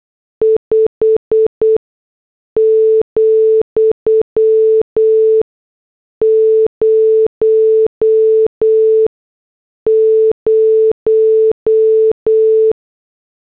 Morse code, heard as a keyed tone, about 8 wpm, 440 Hz. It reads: 5 , 0 0
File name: morse / headlines